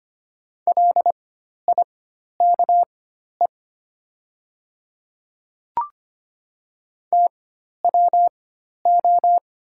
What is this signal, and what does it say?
Morse code, heard as a keyed tone, about 25 wpm, spaced as for 12 wpm, 700 Hz. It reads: LIKE TWO
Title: 4x Character Spacing - Top 100 Words - 25wpm